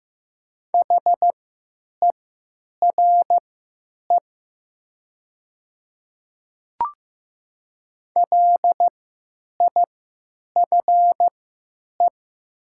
Morse code, heard as a keyed tone, about 15 wpm, spaced as for 9 wpm, 700 Hz. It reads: HERE LIFE
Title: 3x Character Spacing - Top 300 Words - 15wpm